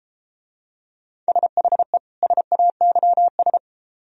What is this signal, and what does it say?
Morse code, heard as a keyed tone, about 31 wpm, 700 Hz. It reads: SHE SAYS